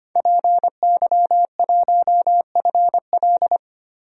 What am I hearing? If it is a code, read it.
Morse code, heard as a keyed tone, about 25 wpm, 700 Hz. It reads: PY1FL
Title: DXCC-100 Call Signs - 5 Characters - Repeated 2x - 25wpm